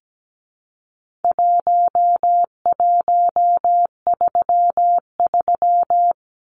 Morse code, heard as a keyed tone, about 17 wpm, 700 Hz. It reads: 1133
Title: Numbers - Contest Style - 4-Digit 17wpm